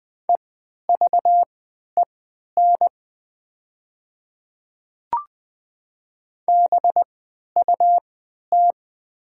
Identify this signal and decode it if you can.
Morse code, heard as a keyed tone, about 20 wpm, spaced as for 11 wpm, 700 Hz. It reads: EVEN BUT
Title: 3x Character Spacing - Top 200 Words - 20wpm